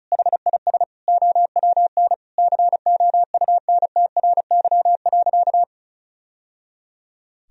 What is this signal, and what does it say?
Morse code, heard as a keyed tone, about 34 wpm, 700 Hz. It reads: HIS OWN COUNTRY.
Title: Sentences from Top 1000 Words - Review Part 1 - 35wpm